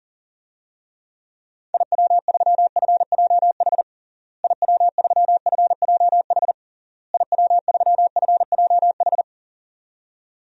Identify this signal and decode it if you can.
Morse code, heard as a keyed tone, about 40 wpm, 700 Hz. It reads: IW3FJH IW3FJH IW3FJH
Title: DXCC-100 Call Signs - 6 Characters - Repeated 3x - 40wpm